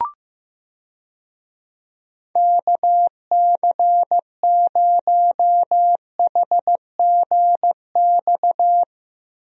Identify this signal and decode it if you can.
Morse code, heard as a keyed tone, about 15 wpm, 700 Hz. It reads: KC0HGX
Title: USA Call Signs - 6 Characters - Repeated 3x - 15wpm